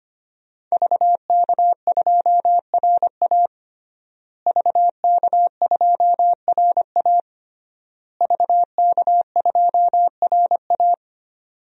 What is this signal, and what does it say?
Morse code, heard as a keyed tone, about 25 wpm, 700 Hz. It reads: VK2RA VK2RA VK2RA
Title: DXCC-100 Call Signs - 5 Characters - Repeated 3x - 25wpm